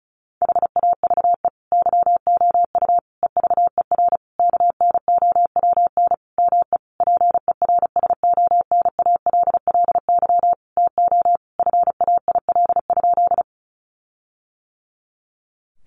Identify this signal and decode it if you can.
Morse code, heard as a keyed tone, about 34 wpm, 700 Hz. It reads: HAVE YOU EVER KNOWN ME PERSONALLY TO FAIL?
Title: Sentences from Top 1400 Words 35wpm